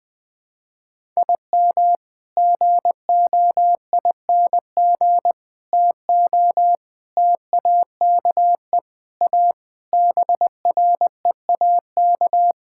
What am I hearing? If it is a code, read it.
Morse code, heard as a keyed tone, about 21 wpm, 700 Hz. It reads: IM GOING TO TAKE A BREAK
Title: Sentences with 5-Word Phrases - Spoken Beginning - 20wpm